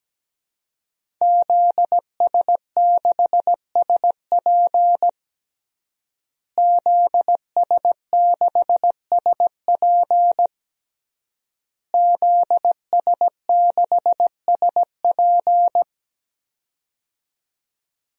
Morse code, heard as a keyed tone, about 17 wpm, 700 Hz. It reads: ZS6SP ZS6SP ZS6SP
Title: DXCC-100 Call Signs - 5 Characters - Repeated 3x - 17wpm